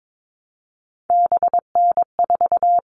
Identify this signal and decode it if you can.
Morse code, heard as a keyed tone, about 22 wpm, 700 Hz. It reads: BN4
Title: All Cut Numbers - 3-Digit 22wpm